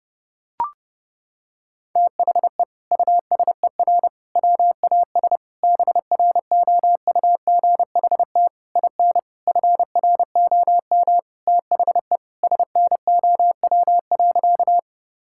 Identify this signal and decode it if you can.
Morse code, heard as a keyed tone, about 30 wpm, 700 Hz. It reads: THE USER WAS BROUGHT IN FROM THE SNOW.
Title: Sentences from Top 1000 Words - Review Part 5 - 30wpm